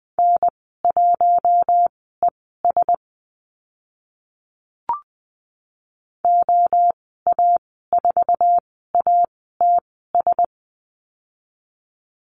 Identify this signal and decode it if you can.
Morse code, heard as a keyed tone, about 20 wpm, spaced as for 15 wpm, 700 Hz. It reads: N1ES OA4ATS
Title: 062 Lesson 11g - Letter h - Call Signs 20wpm